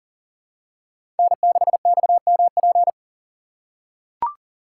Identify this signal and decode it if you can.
Morse code, heard as a keyed tone, about 40 wpm, 700 Hz. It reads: N6XMP
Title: USA Call Signs - 5 Characters - 3x Character Spacing - 40wpm